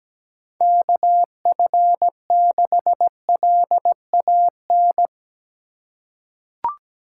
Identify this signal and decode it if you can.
Morse code, heard as a keyed tone, about 17 wpm, 700 Hz. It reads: KF6LAN
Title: USA Call Signs - 6 Characters - 2x Character Spacing - 17wpm